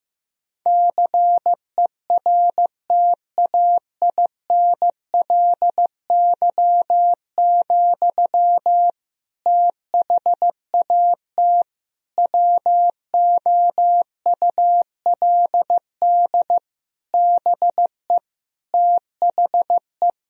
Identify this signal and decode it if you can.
Morse code, heard as a keyed tone, about 15 wpm, 700 Hz. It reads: CERTAINLY, THAT WOULD BE THE
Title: Sentences from Top 1000 Words - Review Part 2 - 15wpm